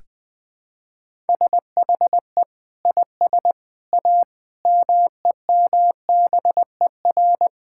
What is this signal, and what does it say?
Morse code, heard as a keyed tone, about 20 wpm, 700 Hz. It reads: SHE IS A MEMBER
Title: Sentences with 3-Word Phrases - 20wpm